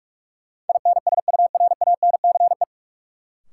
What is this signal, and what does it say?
Morse code, heard as a keyed tone, about 43 wpm, 700 Hz. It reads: INSURANCE
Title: Sentences from Top 1100 Words - New Word Drill - 45wpm